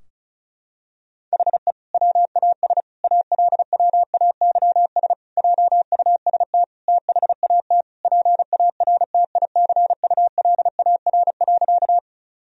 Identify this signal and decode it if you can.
Morse code, heard as a keyed tone, about 34 wpm, 700 Hz. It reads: HE WAS ALWAYS JUST THAT PARTICULAR.
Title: Sentences from Top 1000 Words - Review Part 4 - 35wpm